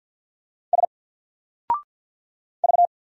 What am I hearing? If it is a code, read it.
Morse code, heard as a keyed tone, about 50 wpm, 700 Hz. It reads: S V
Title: Single Character (Letter:Number:Punctuation:Prosign) 50wpm